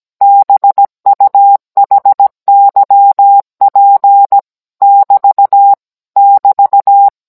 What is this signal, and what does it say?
Morse code, heard as a keyed tone, about 17 wpm, 800 Hz. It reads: BUHYP <BT> <BT>